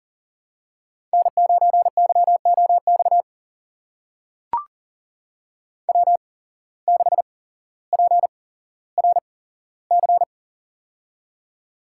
Morse code, heard as a keyed tone, about 40 wpm, spaced as for 17 wpm, 700 Hz. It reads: N9YOX W6PRC
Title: USA Call Signs - 5 Characters - 8x Character Spacing - 40wpm